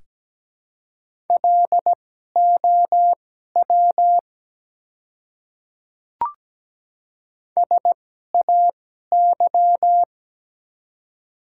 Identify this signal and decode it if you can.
Morse code, heard as a keyed tone, about 17 wpm, spaced as for 12 wpm, 700 Hz. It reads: LOW SAY
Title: 2x Character Spacing - Top 200 Words - 17wpm